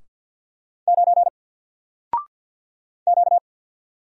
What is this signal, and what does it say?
Morse code, heard as a keyed tone, about 50 wpm, 700 Hz. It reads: 9 Q
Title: Single Letter-Number Alt 50wpm